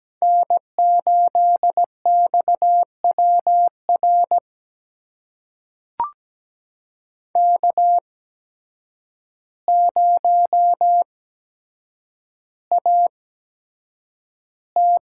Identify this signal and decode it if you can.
Morse code, heard as a keyed tone, about 17 wpm, spaced as for 8 wpm, 700 Hz. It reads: N8XWR K0AT
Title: USA Call Signs - 5 Characters - 8x Character Spacing - 17wpm